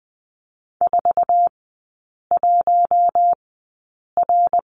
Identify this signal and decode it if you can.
Morse code, heard as a keyed tone, about 20 wpm, spaced as for 8 wpm, 700 Hz. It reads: 41R
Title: 067 Lesson 12e - Letter d - Three Characters 20wpm